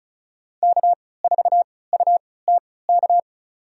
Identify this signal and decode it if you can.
Morse code, heard as a keyed tone, about 34 wpm, spaced as for 20 wpm, 700 Hz. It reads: K4UTK
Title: USA Call Signs - 5 Characters - 3x Character Spacing - 35wpm